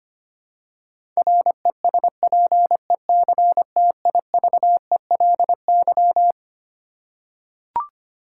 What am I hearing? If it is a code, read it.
Morse code, heard as a keyed tone, about 25 wpm, 700 Hz. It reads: RESPECTIVELY
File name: Twelve Letter Words - 25wpm